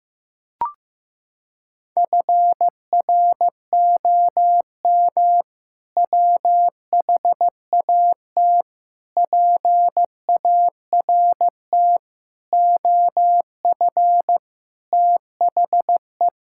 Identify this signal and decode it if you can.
Morse code, heard as a keyed tone, about 15 wpm, 700 Hz. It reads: FROM WHAT PART OF THE
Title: Sentences from Top 1000 Words - Review Part 6 - 15wpm